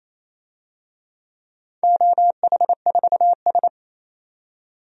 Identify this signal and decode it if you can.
Morse code, heard as a keyed tone, about 28 wpm, 700 Hz. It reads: OH4S
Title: DXCC-100 Call Signs - 4 Characters - 28wpm